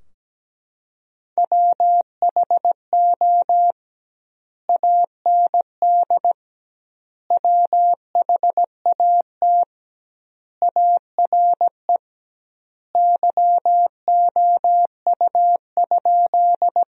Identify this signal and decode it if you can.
Morse code, heard as a keyed tone, about 17 wpm, 700 Hz. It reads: WHO AND WHAT ARE YOU?
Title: 2x Word-Spacing - Sentences from Top 100 Words - 17wpm